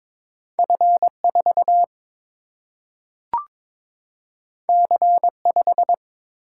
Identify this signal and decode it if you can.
Morse code, heard as a keyed tone, about 22 wpm, 700 Hz. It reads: F4 C5
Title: Single-Letter Single-Number 22wpm